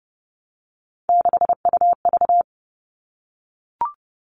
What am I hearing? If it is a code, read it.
Morse code, heard as a keyed tone, about 30 wpm, 700 Hz. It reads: 6UV
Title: All Cut Numbers - 3-Digit 30wpm